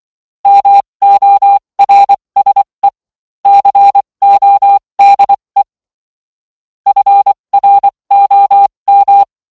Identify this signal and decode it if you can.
Morse code, heard as a keyed tone, about 24 wpm, spaced as for 20 wpm, 800 Hz. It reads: MORSE CODE FROM